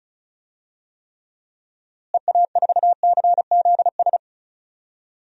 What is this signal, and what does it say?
Morse code, heard as a keyed tone, about 34 wpm, 700 Hz. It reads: EA4CZS
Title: DXCC-100 Call Signs - 6 Characters - Repeated 3x - 35wpm